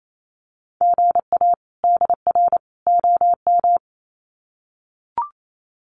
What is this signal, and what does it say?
Morse code, heard as a keyed tone, about 28 wpm, 700 Hz. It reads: GA DR OM
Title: Common QSO Phrases 28wpm